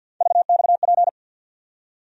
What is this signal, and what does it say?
Morse code, heard as a keyed tone, about 50 wpm, 700 Hz. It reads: VXP
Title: USA Call Sign Postfixes - 3 Letters 50wpm